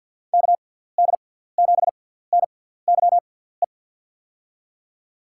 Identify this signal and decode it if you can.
Morse code, heard as a keyed tone, about 50 wpm, spaced as for 17 wpm, 700 Hz. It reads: KD7NYE
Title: USA Call Signs - 6 Characters - 6x Character Spacing - 50wpm